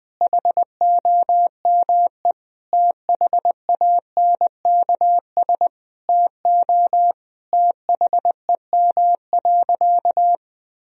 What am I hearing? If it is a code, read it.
Morse code, heard as a keyed tone, about 20 wpm, 700 Hz. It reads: HOME THANKS TO THEM.
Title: Sentences from Top 1000 Words - Review Part 3 - 20wpm